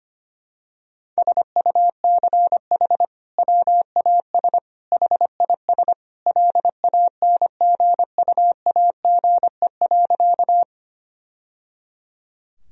Morse code, heard as a keyed tone, about 25 wpm, 700 Hz. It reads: SUCH WAS HIS LANGUAGE.
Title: Sentences from Top 1000 Words - Review Part 3 - 25wpm